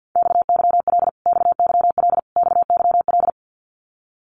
Morse code, heard as a keyed tone, about 48 wpm, 700 Hz. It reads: XYL XYL XYL
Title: Abbreviations Repeated 3x - 50wpm